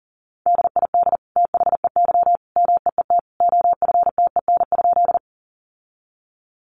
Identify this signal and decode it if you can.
Morse code, heard as a keyed tone, about 37 wpm, 700 Hz. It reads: DID THEY MEET OFTEN?
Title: Sentences from Top 1100 Words 40wpm